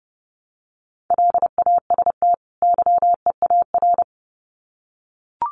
Morse code, heard as a keyed tone, about 30 wpm, 700 Hz. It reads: LAST YEAR
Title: Sets of 2 Words 30wpm